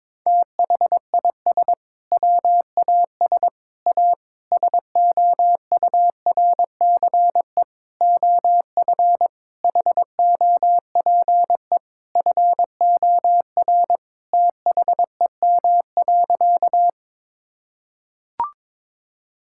Morse code, heard as a keyed tone, about 22 wpm, 700 Hz. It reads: THIS WAS A SOURCE OF HOPE FOR THEM.
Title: Sentences from Top 1000 Words - Review Part 5 - 22wpm